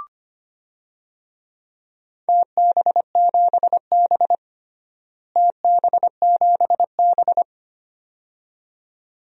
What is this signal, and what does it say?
Morse code, heard as a keyed tone, about 25 wpm, 700 Hz. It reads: TB7B TB7B
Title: DXCC-100 Call Signs - 4 Characters - Repeated 2x - 25wpm